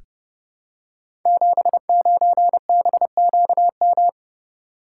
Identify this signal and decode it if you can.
Morse code, heard as a keyed tone, about 30 wpm, 700 Hz. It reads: 79BQM
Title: Code Groups - 5 Letter:Numbers - 30wpm